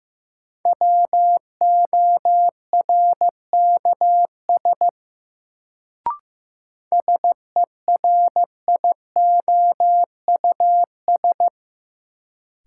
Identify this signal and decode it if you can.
Morse code, heard as a keyed tone, about 15 wpm, 700 Hz. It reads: WORKS SERIOUS
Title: Sentences from Top 1100 Words - New Word Drill Alt - 15wpm